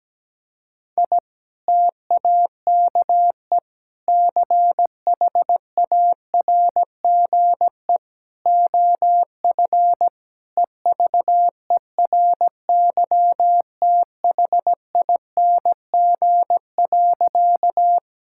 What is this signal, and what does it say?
Morse code, heard as a keyed tone, about 17 wpm, 700 Hz. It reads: I TAKE CHARGE OF EVERYTHING.
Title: Sentences from Top 1000 Words - Review Part 6 - 17wpm